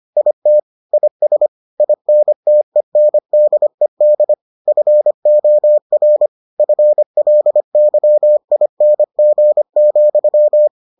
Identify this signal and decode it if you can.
Morse code, heard as a keyed tone, about 25 wpm, 600 Hz. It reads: IT IS INTENDED FOR FLYING,